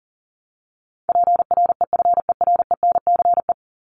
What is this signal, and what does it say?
Morse code, heard as a keyed tone, about 40 wpm, 700 Hz. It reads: PREFERENCE
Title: Top 1251-1500 Words 40wpm